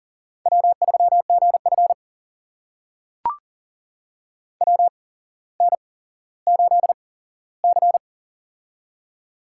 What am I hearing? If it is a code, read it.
Morse code, heard as a keyed tone, about 40 wpm, spaced as for 17 wpm, 700 Hz. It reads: W3GF WN8C
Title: USA Call Signs - 4 Characters - 8x Character Spacing - 40wpm